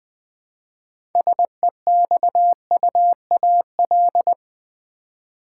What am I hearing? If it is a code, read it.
Morse code, heard as a keyed tone, about 20 wpm, 700 Hz. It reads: SEXUAL
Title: 6 Letter Words - First 3 Letters Given - 20wpm